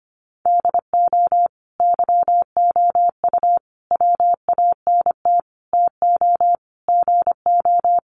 Morse code, heard as a keyed tone, about 25 wpm, 700 Hz. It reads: DO YOU WANT TO GO
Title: Sentences from Top 1400 Words 25wpm